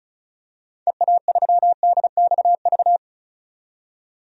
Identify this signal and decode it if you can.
Morse code, heard as a keyed tone, about 36 wpm, 700 Hz. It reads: EA3DXV
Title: DXCC-100 Call Signs - 6 Characters - 35wpm